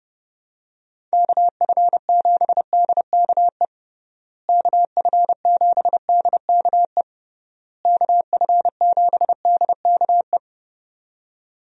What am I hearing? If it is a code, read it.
Morse code, heard as a keyed tone, about 30 wpm, 700 Hz. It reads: KF7DKE KF7DKE KF7DKE
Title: USA Call Signs - 6 Characters - Repeated 3x - 30wpm